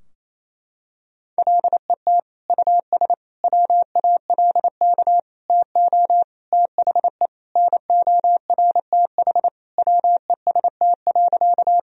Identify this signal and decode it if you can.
Morse code, heard as a keyed tone, about 28 wpm, 700 Hz. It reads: LET US WALK TO THE NORTH WEST.
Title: Sentences from Top 1000 Words - Review Part 2 - 28wpm